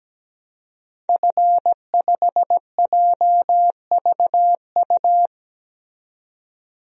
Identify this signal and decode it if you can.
Morse code, heard as a keyed tone, about 17 wpm, 700 Hz. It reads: F5JVU
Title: DXCC-100 Call Signs - 5 Characters - 17wpm